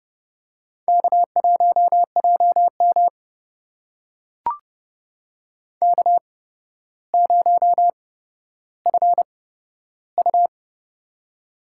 Morse code, heard as a keyed tone, about 30 wpm, spaced as for 13 wpm, 700 Hz. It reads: K1JM K0FU
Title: USA Call Signs - 4 Characters - 8x Character Spacing - 30wpm